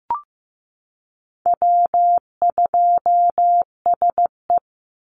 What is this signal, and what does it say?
Morse code, heard as a keyed tone, about 15 wpm, 700 Hz. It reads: W2SE
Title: NA SOTA Chaser Callsigns 15wpm